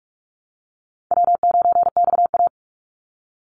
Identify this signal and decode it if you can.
Morse code, heard as a keyed tone, about 44 wpm, 700 Hz. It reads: W9XA
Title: Easy Call Signs Repeated 3x - 45wpm